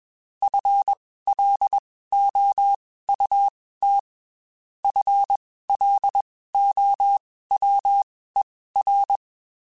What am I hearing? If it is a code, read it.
Morse code, heard as a keyed tone, about 21 wpm, spaced as for 15 wpm, 800 Hz. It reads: FLOUT FLOWER